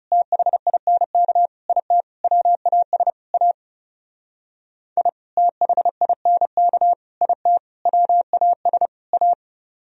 Morse code, heard as a keyed tone, about 37 wpm, 700 Hz. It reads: THINK IT WAS A I THINK IT WAS A
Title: Speed-Racing - Sets of 5 Words 30wpm